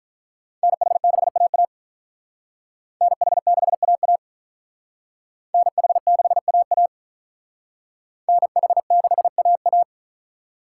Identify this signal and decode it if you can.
Morse code, heard as a keyed tone, about 42 wpm, 700 Hz. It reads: NH6AA NH6AA NH6AA NH6AA
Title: Speed Racing - USA 2x2 Call Signs 35wpm